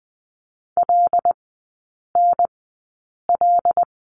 Morse code, heard as a keyed tone, about 20 wpm, spaced as for 8 wpm, 700 Hz. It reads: LNL
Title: 074 Lesson 13e - Letter L - Three Characters 20wpm